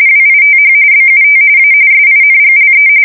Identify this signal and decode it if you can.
RTTY, about 45.45 baud. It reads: WITH BUT A SINGLE